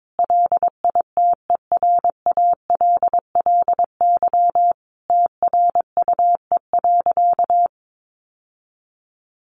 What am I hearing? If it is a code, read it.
Morse code, heard as a keyed tone, about 22 wpm, 700 Hz. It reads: LITERALLY TRUE.
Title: Sentences from Top 1400 Words 22wpm